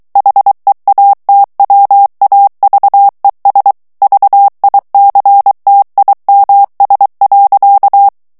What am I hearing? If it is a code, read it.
Morse code, heard as a keyed tone, about 23 wpm, 800 Hz. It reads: HEATWAVES VICTIMS.
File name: morse mp3 / ANGLAIS texte clair 2 vitesse 22 mots minute